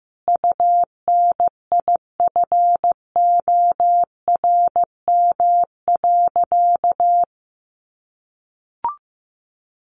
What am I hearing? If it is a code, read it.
Morse code, heard as a keyed tone, about 15 wpm, 700 Hz. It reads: UNIFORM.
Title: Easy Phrases 15wpm